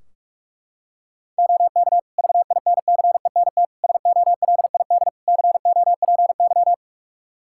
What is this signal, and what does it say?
Morse code, heard as a keyed tone, about 42 wpm, 700 Hz. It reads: OK VINCENT SOLID COPY
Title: Practice QSO - Exchange 3 - More Info - 45wpm